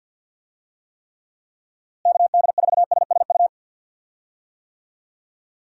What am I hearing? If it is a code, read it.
Morse code, heard as a keyed tone, about 52 wpm, 700 Hz. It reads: KD4SSU
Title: USA Call Signs - 6 Characters - Repeated 3x - 50wpm